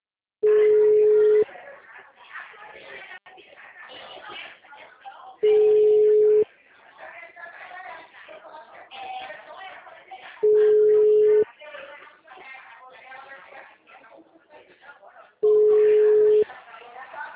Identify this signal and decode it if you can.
Ringback tone (Europe, 425 Hz), with speech.
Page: Leme 2